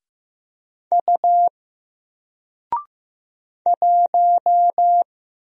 Morse code, heard as a keyed tone, about 15 wpm, 700 Hz. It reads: U 1
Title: Single Character (Letter:Number:Punctuation:Prosign) 15wpm